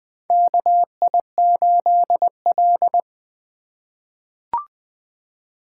Morse code, heard as a keyed tone, about 20 wpm, 700 Hz. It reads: KI8L
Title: USA Call Signs - 2x Character Spacing - 20wpm